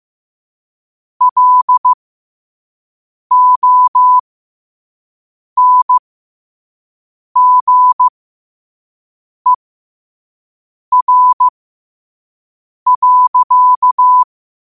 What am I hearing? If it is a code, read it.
Morse code, heard as a keyed tone, about 15 wpm, spaced as for 5 wpm, 1000 Hz. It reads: LONGER.
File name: qotdfiles / 5wpm